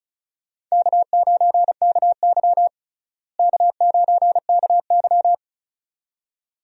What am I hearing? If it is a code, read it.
Morse code, heard as a keyed tone, about 36 wpm, 700 Hz. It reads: K9KY K9KY
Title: USA Call Signs - 4 Characters - Repeated 2x - 35wpm